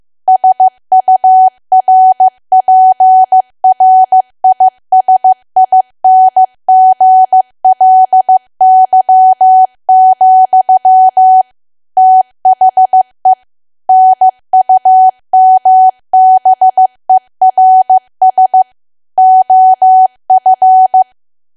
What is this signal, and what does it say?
Morse code, heard as a keyed tone, about 15 wpm, 750 Hz. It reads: SURPRISINGLY, THE NUMBERS OF
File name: cwa / W1AW-Level2 / 15WPM